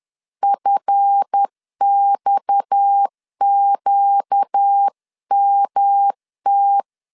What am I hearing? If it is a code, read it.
Morse code, heard as a keyed tone, about 10 wpm, 800 Hz. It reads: FXQMT
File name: CW / DET / 1 9 1500C S A 10wpm 4511